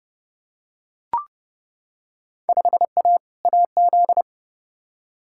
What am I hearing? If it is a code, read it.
Morse code, heard as a keyed tone, about 30 wpm, 700 Hz. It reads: 5A AZ
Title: Field Day Exchanges - 30wpm